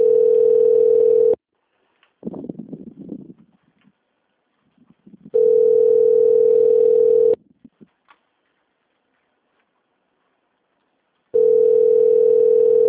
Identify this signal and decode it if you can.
Ringback tone (North America), 440 + 480 Hz.